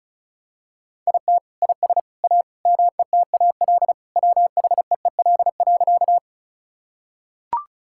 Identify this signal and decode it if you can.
Morse code, heard as a keyed tone, about 34 wpm, 700 Hz. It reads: IT IS A METAL WHEEL.
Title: Sentences from Top 1000 Words - Review Part 6 - 35wpm